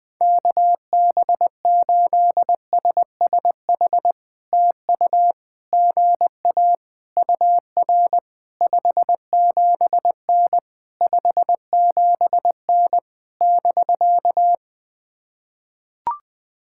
Morse code, heard as a keyed tone, about 20 wpm, 700 Hz. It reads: KB8SSH TU GA UR 57N 57N <BK>
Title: POTA QSO Practice - 20wpm